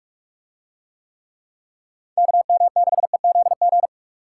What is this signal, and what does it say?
Morse code, heard as a keyed tone, about 45 wpm, 700 Hz. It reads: KM6EZG